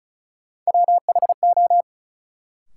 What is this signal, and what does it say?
Morse code, heard as a keyed tone, about 36 wpm, 700 Hz. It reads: WHO
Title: Sets of 2 Words - New Word Drill - 35wpm